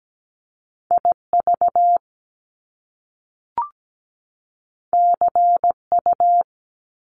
Morse code, heard as a keyed tone, about 17 wpm, 700 Hz. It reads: IV CU
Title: Top Two Letter Combinations in Words 17wpm